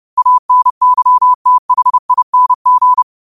Morse code, heard as a keyed tone, about 30 wpm, 1000 Hz. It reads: ANYTHING
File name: qotdfiles / 30wpm